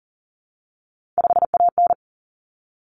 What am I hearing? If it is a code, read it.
Morse code, heard as a keyed tone, about 40 wpm, 700 Hz. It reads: 5AN
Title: Common Cut Numbers - 3-Digit 40wpm